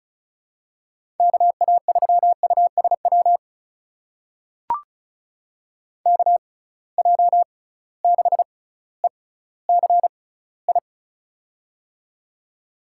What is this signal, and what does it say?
Morse code, heard as a keyed tone, about 36 wpm, spaced as for 18 wpm, 700 Hz. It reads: KA3USW KJ6ECI
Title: USA Call Signs - 6 Characters - 6x Character Spacing - 35wpm